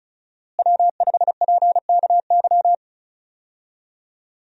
Morse code, heard as a keyed tone, about 36 wpm, 700 Hz. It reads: W5PKY
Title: USA Call Signs - 5 Characters - Repeated 3x - 35wpm